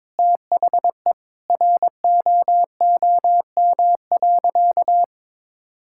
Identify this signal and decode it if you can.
Morse code, heard as a keyed tone, about 22 wpm, 700 Hz. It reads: THE ROOM.
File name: practice / 22wpm / Sentences with 3-Word Phrases - 22wpm